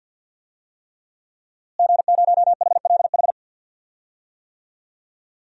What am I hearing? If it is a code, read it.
Morse code, heard as a keyed tone, about 48 wpm, 700 Hz. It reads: G0HLH
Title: DXCC-100 Call Signs - 5 Characters - Repeated 3x - 50wpm